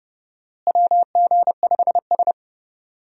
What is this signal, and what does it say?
Morse code, heard as a keyed tone, about 30 wpm, 700 Hz. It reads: WG5S
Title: USA Call Signs - 4 Characters - Repeated 2x - 30wpm